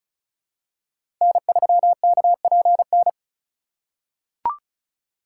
Morse code, heard as a keyed tone, about 34 wpm, 700 Hz. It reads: N3KPN
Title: USA Call Signs - 5 Characters - 3x Character Spacing - 35wpm